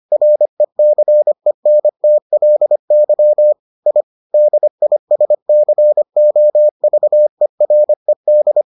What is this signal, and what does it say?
Morse code, heard as a keyed tone, about 25 wpm, 600 Hz. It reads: RECENTLY I DISCOVERED